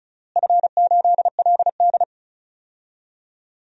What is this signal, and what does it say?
Morse code, heard as a keyed tone, about 34 wpm, 700 Hz. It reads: F8LD
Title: DXCC-100 Call Signs - 4 Characters - Repeated 3x - 35wpm